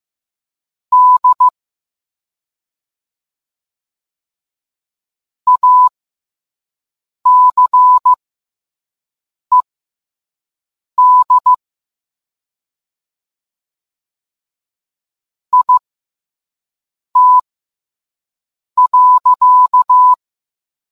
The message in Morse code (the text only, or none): D ACED IT.